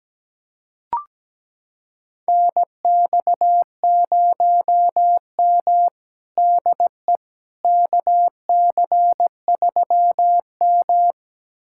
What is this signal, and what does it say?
Morse code, heard as a keyed tone, about 17 wpm, 700 Hz. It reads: NX0M DE KC3M
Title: USA Callsign DE Callsign - 4 Characters - 17wpm